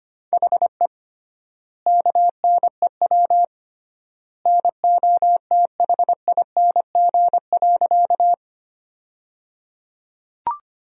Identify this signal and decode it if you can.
Morse code, heard as a keyed tone, about 25 wpm, 700 Hz. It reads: HE KNEW NOTHING.
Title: 3x Word-Spacing - Sentences from Top 500 Words - 25wpm